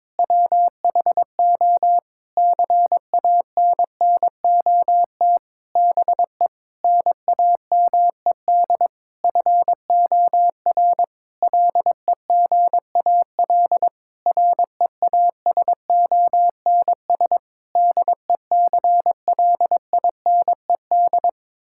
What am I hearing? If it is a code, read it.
Morse code, heard as a keyed tone, about 22 wpm, 700 Hz. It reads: WHO CANNOT BE NAMED FOR LEGAL REASONS DECLINED
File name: practice / 22wpm / Sentences with 8-Word Phrases - 22wpm